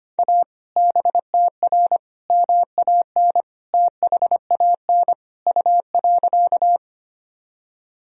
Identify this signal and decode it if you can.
Morse code, heard as a keyed tone, about 25 wpm, 700 Hz. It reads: A BTR MAN THAN U.
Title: Sentences from Top 400 Words and Abbreviations 25wpm